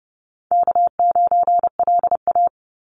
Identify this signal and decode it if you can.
Morse code, heard as a keyed tone, about 32 wpm, 700 Hz. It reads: K9LA
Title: Call Signs - 30wpm